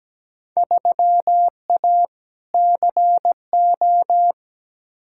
Morse code, heard as a keyed tone, about 17 wpm, 700 Hz. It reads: 3A CO
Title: Field Day Exchanges Encore - 17wpm